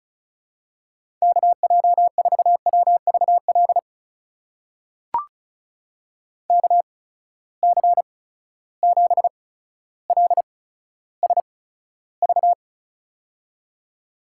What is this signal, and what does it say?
Morse code, heard as a keyed tone, about 36 wpm, spaced as for 15 wpm, 700 Hz. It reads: KJ4WVL KC7LSV
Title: USA Call Signs - 6 Characters - 8x Character Spacing - 35wpm